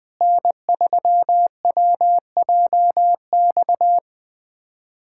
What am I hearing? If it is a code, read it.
Morse code, heard as a keyed tone, about 20 wpm, 700 Hz. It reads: N3WJX
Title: USA Callsign DE Callsign 20wpm